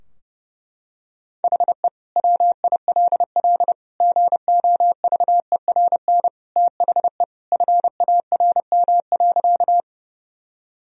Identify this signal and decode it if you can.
Morse code, heard as a keyed tone, about 32 wpm, 700 Hz. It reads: HE WILL GOVERN THE FARM.
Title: Sentences from Top 1000 Words - Review Part 2 - 30wpm